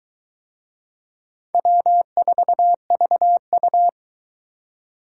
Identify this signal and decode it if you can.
Morse code, heard as a keyed tone, about 22 wpm, 700 Hz. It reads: W4VU